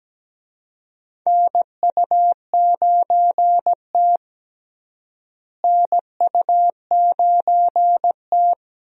Morse code, heard as a keyed tone, about 17 wpm, 700 Hz. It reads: NU9T NU9T
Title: USA Call Signs - 4 Characters - Repeated 2x - 17wpm